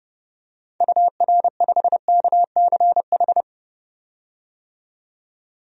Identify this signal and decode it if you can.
Morse code, heard as a keyed tone, about 30 wpm, 700 Hz. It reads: UR5KCH
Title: DXCC-100 Call Signs - 6 Characters - 30wpm